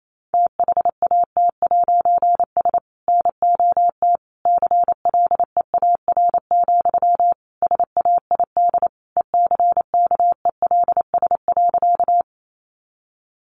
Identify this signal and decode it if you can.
Morse code, heard as a keyed tone, about 28 wpm, 700 Hz. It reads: THAT'S NOT CLEAR, SAID ECKELS.
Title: A Sound Of Thunder 28wpm